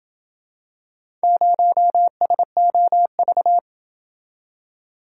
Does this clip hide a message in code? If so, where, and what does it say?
Yes: Morse code from 1.2 to 3.6 s, heard as a keyed tone, about 28 wpm, 700 Hz. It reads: 0SOV